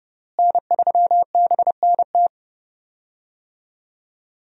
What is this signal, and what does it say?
Morse code, heard as a keyed tone, about 30 wpm, 700 Hz. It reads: N3BNT
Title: USA Call Signs - 5 Characters - Repeated 2x - 30wpm